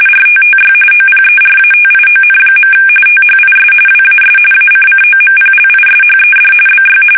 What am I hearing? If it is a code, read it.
RTTY, about 45.45 baud. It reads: RTTY OPERATION. THIS WAS ACCOMPLISHED BY
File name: fsk850 / teletype-RTTY